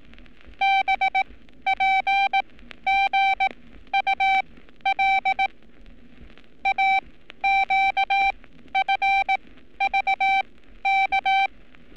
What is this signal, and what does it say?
Morse code, heard as a keyed tone, about 18 wpm, spaced as for 12 wpm, 780 Hz. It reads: BPGUL AQFVK
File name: training / NAVPERS 11224-11235 / NAVPERS 11227 RA RB